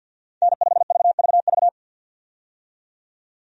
Morse code, heard as a keyed tone, about 50 wpm, 700 Hz. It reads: N5VVV
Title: USA Call Signs - 5 Characters - 50wpm